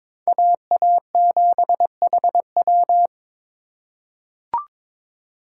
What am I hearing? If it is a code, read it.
Morse code, heard as a keyed tone, about 22 wpm, 700 Hz. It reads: AA7HW
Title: USA Call Signs - 5 Characters - 6x Character Spacing - 22wpm